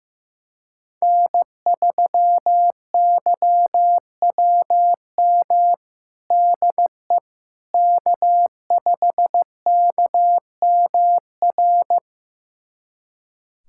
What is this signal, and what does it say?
Morse code, heard as a keyed tone, about 15 wpm, 700 Hz. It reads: N3YWM DE K5KMR
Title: USA Callsign DE Callsign - 5 Characters - 15wpm